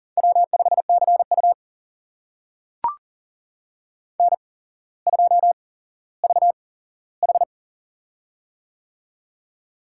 Morse code, heard as a keyed tone, about 40 wpm, spaced as for 17 wpm, 700 Hz. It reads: W5CU N2VH
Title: USA Call Signs - 4 Characters - 8x Character Spacing - 40wpm